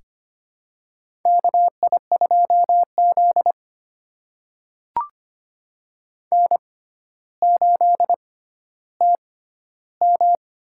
Morse code, heard as a keyed tone, about 25 wpm, spaced as for 13 wpm, 700 Hz. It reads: KI2Z N8TM
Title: USA Call Signs - 4 Characters - 6x Character Spacing - 25wpm